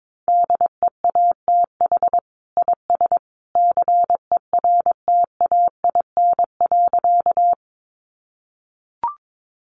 Morse code, heard as a keyed tone, about 22 wpm, 700 Hz. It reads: DEATH IS CERTAIN.
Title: Sentences from Top 1100 Words 22wpm